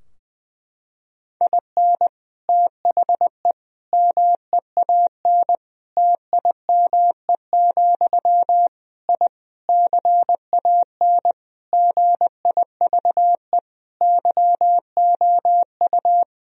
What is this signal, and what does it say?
Morse code, heard as a keyed tone, about 20 wpm, 700 Hz. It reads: IN THE MEAN TIME, I CAN GIVE YOU
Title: Sentences from Top 1000 Words - Review Part 3 - 20wpm